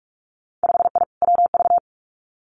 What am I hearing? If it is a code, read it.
Morse code, heard as a keyed tone, about 46 wpm, 700 Hz. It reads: 5I WV
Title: Winter Field Day Exchanges Encore 45wpm